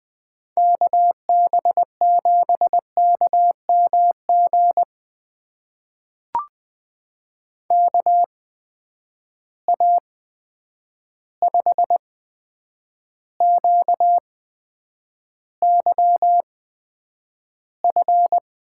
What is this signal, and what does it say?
Morse code, heard as a keyed tone, about 20 wpm, spaced as for 9 wpm, 700 Hz. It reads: KB7KMG KA5QYF
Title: USA Call Signs - 6 Characters - 8x Character Spacing - 20wpm